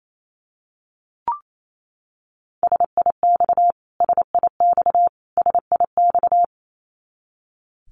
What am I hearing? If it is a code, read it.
Morse code, heard as a keyed tone, about 28 wpm, 700 Hz. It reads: SIX SIX SIX
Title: Top 500 Words Repeated 28wpm